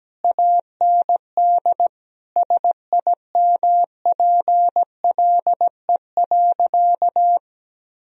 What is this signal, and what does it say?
Morse code, heard as a keyed tone, about 17 wpm, 700 Hz. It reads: AND SIMPLE.
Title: Sentences from Top 1000 Words - Review Part 3 - 17wpm